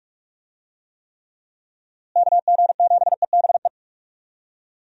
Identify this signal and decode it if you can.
Morse code, heard as a keyed tone, about 46 wpm, 700 Hz. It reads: KG7EBE
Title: USA Call Signs - 6 Characters - Repeated 3x - 45wpm